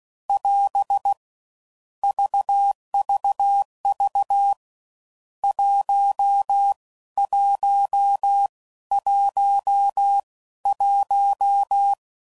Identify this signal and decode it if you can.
Morse code, heard as a keyed tone, about 16 wpm, spaced as for 13 wpm, 800 Hz. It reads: <AS> VVV 1111
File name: Mixed / Misc / Numbs